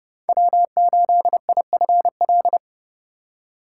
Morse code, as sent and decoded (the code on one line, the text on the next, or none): .-- ---.. .. ..-. .-..
W8IFL